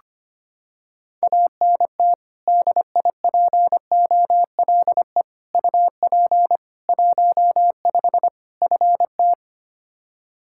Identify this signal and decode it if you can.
Morse code, heard as a keyed tone, about 25 wpm, 700 Hz. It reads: ANT DIPOLE UP 15 FT
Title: Practice QSO - Exchange 2 - Info - 25wpm